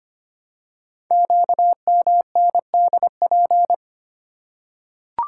Morse code, heard as a keyed tone, about 25 wpm, 700 Hz. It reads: QMNDP
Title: Code Groups - 5 Letters - 25wpm